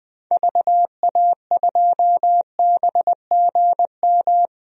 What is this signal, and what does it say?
Morse code, heard as a keyed tone, about 20 wpm, 700 Hz. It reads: VA2BGM
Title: DXCC-100 Call Signs - 6 Characters - Repeated 2x - 20wpm